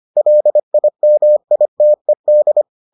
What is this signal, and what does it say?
Morse code, heard as a keyed tone, about 25 wpm, 600 Hz. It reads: LIMITED